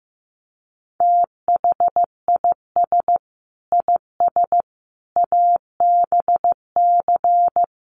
Morse code, heard as a keyed tone, about 15 wpm, 700 Hz. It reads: THIS IS ABC
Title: Sets of 5 Words 15wpm